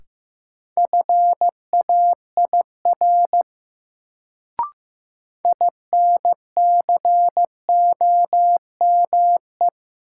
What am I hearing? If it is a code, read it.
Morse code, heard as a keyed tone, about 16 wpm, 700 Hz. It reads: FAIR INCOME
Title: Sentences from Top 1000 Words - Review Part 5 - New Word Drill - 15wpm